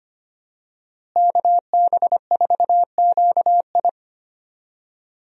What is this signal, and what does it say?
Morse code, heard as a keyed tone, about 25 wpm, 700 Hz. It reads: KB4QI
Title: USA Call Signs - 5 Characters - Repeated 3x - 25wpm